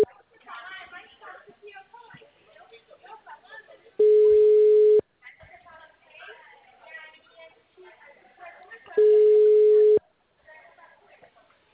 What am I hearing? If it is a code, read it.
Ringback tone (Europe, 425 Hz), with speech.
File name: out- -2003- - - .160